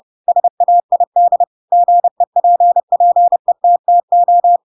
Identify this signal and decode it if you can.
Morse code, heard as a keyed tone, about 30 wpm, 700 Hz. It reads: SAID GEPPETTO
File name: Pinocchio / Pinocchio ch3 30 30 ik1hge